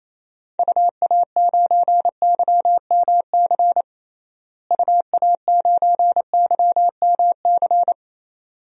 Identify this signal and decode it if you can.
Morse code, heard as a keyed tone, about 28 wpm, 700 Hz. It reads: UA9YMC UA9YMC
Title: DXCC-100 Call Signs - 6 Characters - Repeated 2x - 28wpm